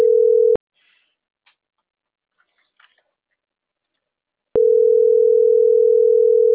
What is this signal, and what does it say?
Ringback tone (North America), 440 + 480 Hz.